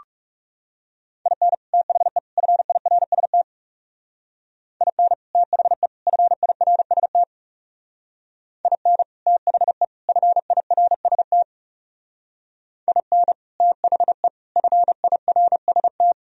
Morse code, heard as a keyed tone, about 39 wpm, 700 Hz. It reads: IN THE FIRST IN THE FIRST IN THE FIRST IN THE FIRST